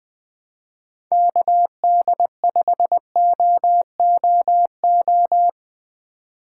Morse code, heard as a keyed tone, about 20 wpm, 700 Hz. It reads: KD5OOO
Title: USA Call Signs - 6 Characters - 4x Character Spacing - 20wpm